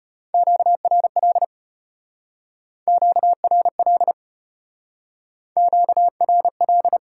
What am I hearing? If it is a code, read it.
Morse code, heard as a keyed tone, about 32 wpm, 700 Hz. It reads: QRL QRL QRL